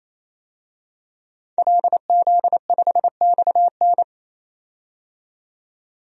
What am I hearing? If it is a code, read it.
Morse code, heard as a keyed tone, about 28 wpm, 700 Hz. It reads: LZ5XN
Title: DXCC-100 Call Signs - 5 Characters - 28wpm